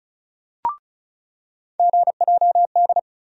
Morse code, heard as a keyed tone, about 36 wpm, 700 Hz. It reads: GJD
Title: USA Call Sign Postfixes - 3 Letters 35wpm